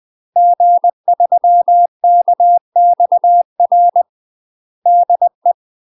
Morse code, heard as a keyed tone, about 20 wpm, 700 Hz. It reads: G3KXR DE
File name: UK MASTER QSOs TRAINING / 20WPM / UK MASTER QSOs 0038